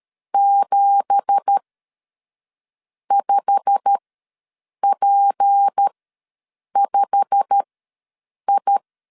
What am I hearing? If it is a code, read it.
Morse code, heard as a keyed tone, about 13 wpm, spaced as for 6 wpm, 800 Hz. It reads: 75P5I
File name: LEZIONI / 10 SIGNS / 10D COMA ws 12wpm 3618